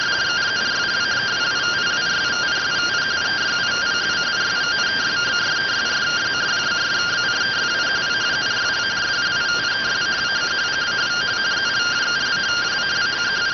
Two-tone teleprinter signal, about 170 Hz shift.